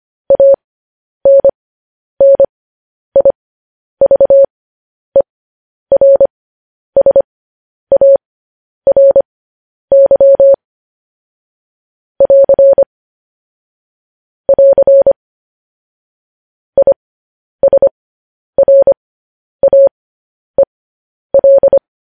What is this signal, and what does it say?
Morse code, heard as a keyed tone, about 25 wpm, spaced as for 10 wpm, 550 Hz. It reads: ANNIVERSARY <AR> <AR> ISRAEL
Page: News Headlines in Morse Code at 10 WPM